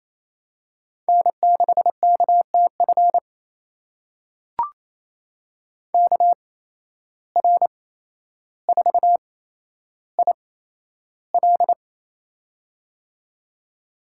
Morse code, heard as a keyed tone, about 27 wpm, spaced as for 12 wpm, 700 Hz. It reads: N6KTF KR4IL
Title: USA Call Signs - 8x Character Spacing - 28wpm